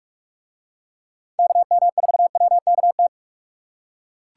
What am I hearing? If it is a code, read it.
Morse code, heard as a keyed tone, about 41 wpm, 700 Hz. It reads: KM4WKT